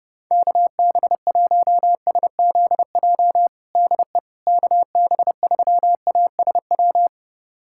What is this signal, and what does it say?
Morse code, heard as a keyed tone, about 30 wpm, 700 Hz. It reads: KB1SZJ DE KB3ASW
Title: USA Callsign DE Callsign - 6 Characters - 30wpm